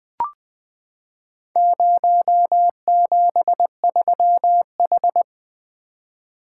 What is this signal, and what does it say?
Morse code, heard as a keyed tone, about 20 wpm, 700 Hz. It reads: 073H
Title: Four Letter-Number Combinations - 20wpm